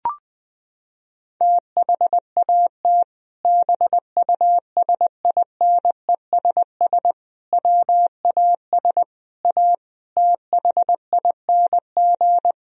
Morse code, heard as a keyed tone, about 21 wpm, 700 Hz. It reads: THAT BUSINESS WAS A THING
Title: Sentences from Top 1000 Words - Review Part 3 - 20wpm